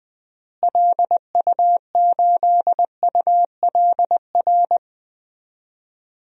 Morse code, heard as a keyed tone, about 21 wpm, 700 Hz. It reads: LU8ULR
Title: DXCC-100 Call Signs - 6 Characters - Repeated 3x - 20wpm